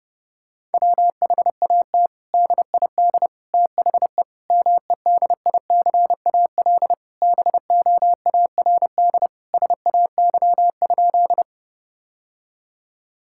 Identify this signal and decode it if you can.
Morse code, heard as a keyed tone, about 32 wpm, 700 Hz. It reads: WHAT DID THE MEDICAL BOARD SAY?
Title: Sentences from Top 1000 Words - Review Part 5 - 30wpm